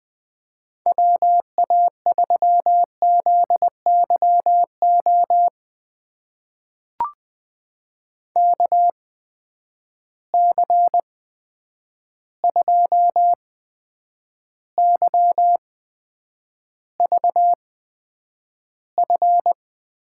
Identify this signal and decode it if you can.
Morse code, heard as a keyed tone, about 20 wpm, spaced as for 9 wpm, 700 Hz. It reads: WA3ZYO KC2YVF
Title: USA Call Signs - 6 Characters - 8x Character Spacing - 20wpm